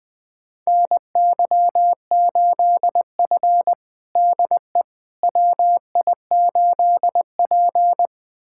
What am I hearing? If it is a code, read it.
Morse code, heard as a keyed tone, about 21 wpm, 700 Hz. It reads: NY8F DE WI8P
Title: USA Callsign DE Callsign - 4 Characters - 20wpm